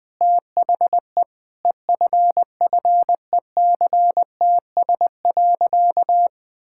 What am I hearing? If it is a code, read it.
Morse code, heard as a keyed tone, about 20 wpm, 700 Hz. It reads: THE EFFECTS.
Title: Sentences with 5-Word Phrases - 20wpm